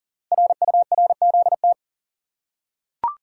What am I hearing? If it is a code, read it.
Morse code, heard as a keyed tone, about 40 wpm, 700 Hz. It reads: RURZT
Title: Code Groups - 5 Letters - 40wpm